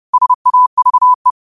Morse code, heard as a keyed tone, about 30 wpm, 1000 Hz. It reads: SAVE
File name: qotdfiles / 30wpm